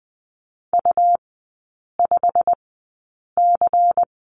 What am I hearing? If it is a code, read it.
Morse code, heard as a keyed tone, about 20 wpm, spaced as for 8 wpm, 690 Hz. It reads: U5C
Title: 145 Lesson 24e - Letter y - Three Characters 20wpm